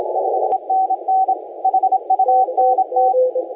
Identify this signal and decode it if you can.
Morse code, heard as a keyed tone, about 27 wpm, 760 Hz, with radio noise. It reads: 5NN HUNT
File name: la4c / sac-cw-2018 / CONTEST AUDIO 1700Z